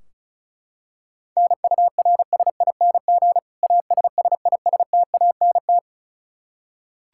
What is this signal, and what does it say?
Morse code, heard as a keyed tone, about 36 wpm, 700 Hz. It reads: NURSING ASSISTANT
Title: Practice QSO - Exchange 3 - More Info - 35wpm